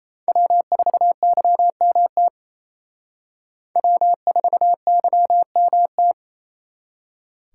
Morse code, heard as a keyed tone, about 35 wpm, 700 Hz. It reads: W4YMT W4YMT
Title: Speed Racing - USA Call Signs 28wpm